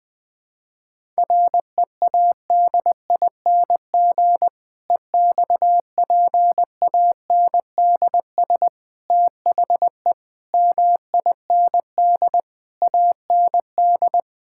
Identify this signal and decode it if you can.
Morse code, heard as a keyed tone, about 20 wpm, 700 Hz. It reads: READING EXPANDS THE MIND AND
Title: Difficult Sentences with Hints - 20wpm